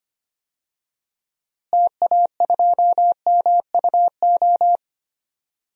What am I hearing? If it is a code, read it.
Morse code, heard as a keyed tone, about 25 wpm, 700 Hz. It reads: TA2MUO
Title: DXCC-100 Call Signs - 6 Characters - Repeated 3x - 25wpm